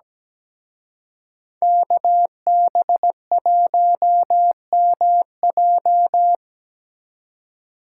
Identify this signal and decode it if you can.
Morse code, heard as a keyed tone, about 20 wpm, 700 Hz. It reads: KB1MJ
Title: Speed Racing - USA 2x2 Call Signs 17wpm